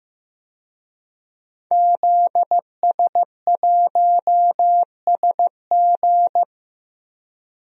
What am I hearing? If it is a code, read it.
Morse code, heard as a keyed tone, about 15 wpm, 700 Hz. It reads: ZS1SG
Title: DXCC-100 Call Signs - 5 Characters - Repeated 2x - 15wpm